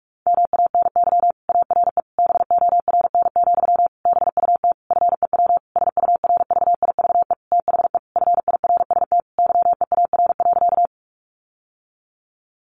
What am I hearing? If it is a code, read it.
Morse code, heard as a keyed tone, about 46 wpm, 700 Hz. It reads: MANY ARE BORN, BUT FEW SURVIVE THE FIRST YEAR.
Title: Sentences from Top 1300 Words 45wpm